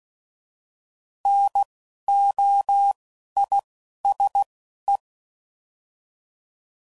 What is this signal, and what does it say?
Morse code, heard as a keyed tone, about 16 wpm, 800 Hz. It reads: N O I S E
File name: Prefixes / By First Letter / Ns